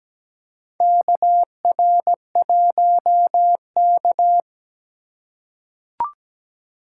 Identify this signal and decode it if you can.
Morse code, heard as a keyed tone, about 17 wpm, 700 Hz. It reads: KR1K
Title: USA Call Signs - 4 Characters - 3x Character Spacing - 17wpm